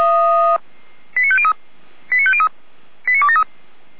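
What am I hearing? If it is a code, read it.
Five-tone selective call (ZVEI), heard as quick runs of five tones, tones of about 70 ms. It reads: 89503 89503 89273
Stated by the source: ZVEI five-tone selective call received on an SDR.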